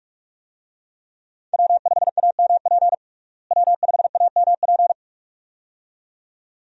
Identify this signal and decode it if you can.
Morse code, heard as a keyed tone, about 46 wpm, 700 Hz. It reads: W5AMP W5AMP
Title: USA Call Signs - 5 Characters - Repeated 2x - 45wpm